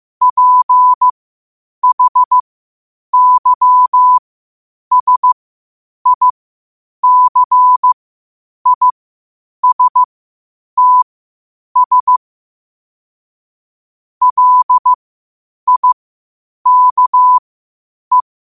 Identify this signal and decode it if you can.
Morse code, heard as a keyed tone, about 15 wpm, spaced as for 8 wpm, 1000 Hz. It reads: PHYSICISTS LIKE
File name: qotdfiles / 8wpm